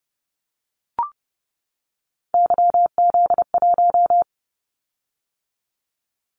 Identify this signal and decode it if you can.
Morse code, heard as a keyed tone, about 30 wpm, 700 Hz. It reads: YZ1
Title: Callsign Prefixes for Top 25 DXCC - 30wpm